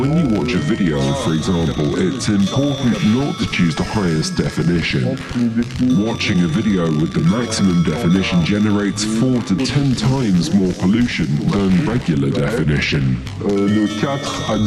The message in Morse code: none

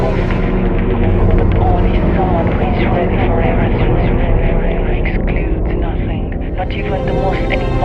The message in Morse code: none

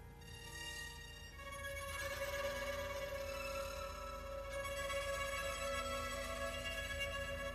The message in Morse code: none

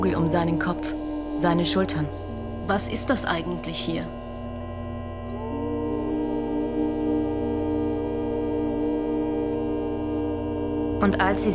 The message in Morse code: none

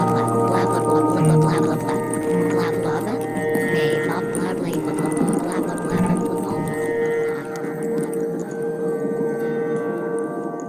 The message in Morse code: none

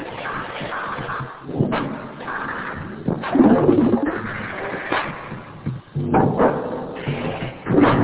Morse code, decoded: none

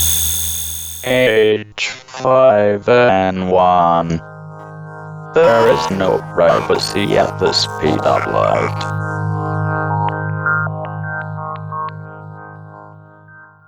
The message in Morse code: none

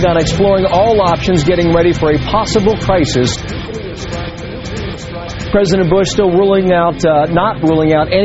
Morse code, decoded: none

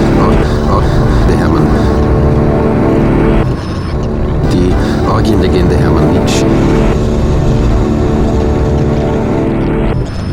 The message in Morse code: none

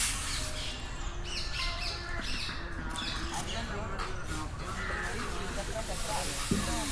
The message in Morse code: none

